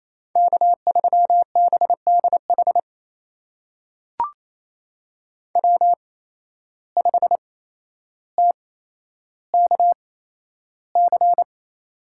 Morse code, heard as a keyed tone, about 28 wpm, spaced as for 12 wpm, 700 Hz. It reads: K3BDH W5TKC
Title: USA Call Signs - 5 Characters - 8x Character Spacing - 28wpm